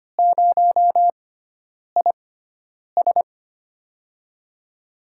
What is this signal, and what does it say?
Morse code, heard as a keyed tone, about 25 wpm, spaced as for 9 wpm, 700 Hz. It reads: 0IS
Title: USA Call Signs - 6x Character Spacing - 25wpm